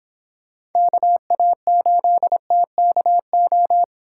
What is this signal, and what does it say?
Morse code, heard as a keyed tone, about 29 wpm, 700 Hz. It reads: KA8TKO